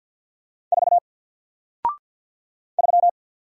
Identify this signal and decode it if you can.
Morse code, heard as a keyed tone, about 50 wpm, 700 Hz. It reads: 4 3
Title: Single Letter-Number Alt 50wpm